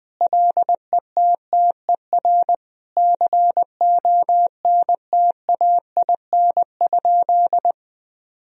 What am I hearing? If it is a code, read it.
Morse code, heard as a keyed tone, about 20 wpm, 700 Hz. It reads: LETTER CONTAIN?
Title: Sentences from Top 1000 Words - Review Part 2 - 20wpm